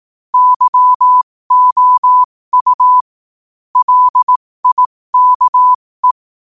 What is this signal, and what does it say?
Morse code, heard as a keyed tone, about 18 wpm, spaced as for 15 wpm, 1000 Hz. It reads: YOU LIKE
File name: qotdfiles / 15wpm